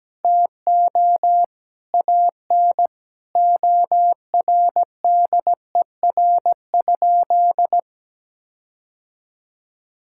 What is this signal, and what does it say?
Morse code, heard as a keyed tone, about 17 wpm, 700 Hz. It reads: TO AN ORDER?
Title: Sentences from Top 1600 Words 17wpm